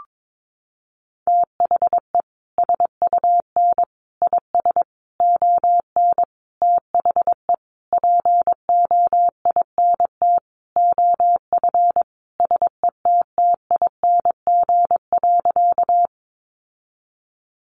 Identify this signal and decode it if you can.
Morse code, heard as a keyed tone, about 22 wpm, 700 Hz. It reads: THE SUN IS ON THE POINT OF SETTING.
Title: Sentences from Top 1300 Words 22wpm